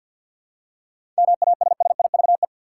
Morse code, heard as a keyed tone, about 50 wpm, 700 Hz. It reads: MASSIVE
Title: Seven Letter Words - 50wpm